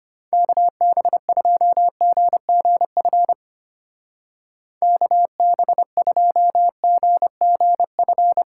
Morse code, heard as a keyed tone, about 31 wpm, 700 Hz. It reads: KB2GGF KB2GGF